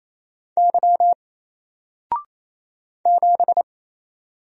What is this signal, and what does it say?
Morse code, heard as a keyed tone, about 27 wpm, 700 Hz. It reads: Y 7
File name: Single-Letter-Number-28wpm